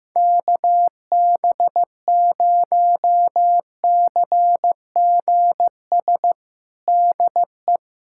Morse code, heard as a keyed tone, about 15 wpm, 700 Hz. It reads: KB0CGS DE
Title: USA Callsign DE Callsign - 6 Characters - 15wpm